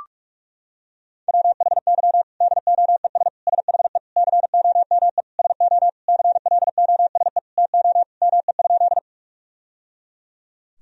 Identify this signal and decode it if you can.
Morse code, heard as a keyed tone, about 41 wpm, 700 Hz. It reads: WHY DOES SHE COME SO CLOSE TO ME?
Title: Sentences from Top 1000 Words - Review Part 1 - 45wpm